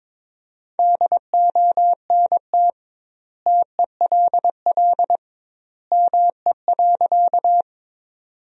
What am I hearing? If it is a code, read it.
Morse code, heard as a keyed tone, about 22 wpm, 700 Hz. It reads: DONT TELL ME.
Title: 2x Word-Spacing - Sentences from Top 300 Words - 22wpm